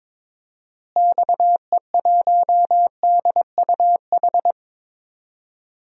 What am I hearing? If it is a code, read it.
Morse code, heard as a keyed tone, about 23 wpm, 700 Hz. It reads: XE1DUH
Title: DXCC-100 Call Signs - 6 Characters - Repeated 2x - 22wpm